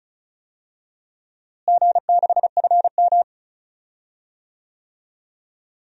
Morse code, heard as a keyed tone, about 34 wpm, 700 Hz. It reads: G6FM
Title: DXCC-100 Call Signs - 4 Characters - Repeated 2x - 35wpm